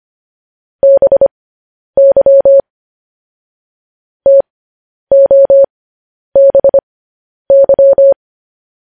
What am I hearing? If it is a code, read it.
Morse code, heard as a keyed tone, about 25 wpm, spaced as for 10 wpm, 550 Hz. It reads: BY TOBY
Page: News Headlines in Morse Code at 10 WPM